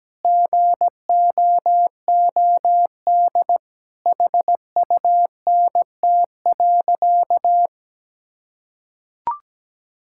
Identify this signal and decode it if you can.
Morse code, heard as a keyed tone, about 17 wpm, 700 Hz. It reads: GOOD HUNT.
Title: Sentences from Top 1600 Words 17wpm